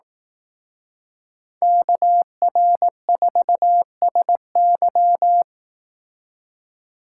Morse code, heard as a keyed tone, about 18 wpm, 700 Hz. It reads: KR4SY
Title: Speed Racing - USA Call Signs 15wpm